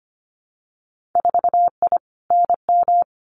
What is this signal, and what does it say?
Morse code, heard as a keyed tone, about 25 wpm, 700 Hz. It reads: 4I NM
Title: Winter Field Day Exchanges Encore 25wpm